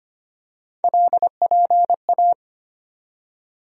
Morse code, heard as a keyed tone, about 25 wpm, 700 Hz. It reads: LPA
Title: USA Call Sign Postfixes - 3 Letters 25wpm